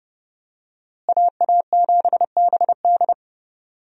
Morse code, heard as a keyed tone, about 30 wpm, 700 Hz. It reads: AA7BD
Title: USA Call Signs - 5 Characters - Repeated 2x - 30wpm